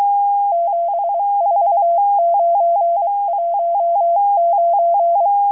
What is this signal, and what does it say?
Morse code, heard as a keyed tone, about 23 wpm, 680 Hz. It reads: 74919